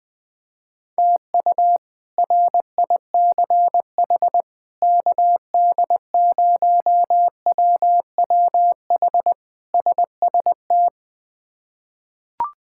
Morse code, heard as a keyed tone, about 21 wpm, 700 Hz. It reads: TU RICH KD0WWH SST
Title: K1USN SST QSO Practice - 20wpm